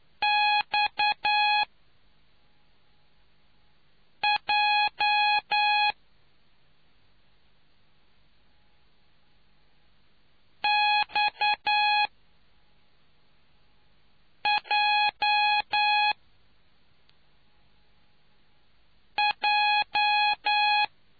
Morse code, heard as a keyed tone, about 9 wpm, 830 Hz. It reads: X J X J J